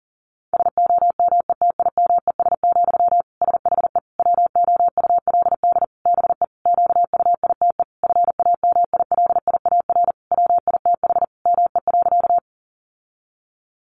Morse code, heard as a keyed tone, about 40 wpm, 700 Hz. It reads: SOMETIMES, SHE WOULD BE QUITE FAMILIAR WITH ME.
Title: Sentences from Top 1000 Words 40wpm